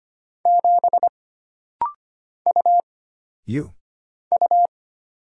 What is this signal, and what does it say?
Morse code, heard as a keyed tone, about 25 wpm, 700 Hz. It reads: 7 U U
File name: Single Character (Letter:Number:Punctuation:Prosign) - Rapid-Fire - 25wpm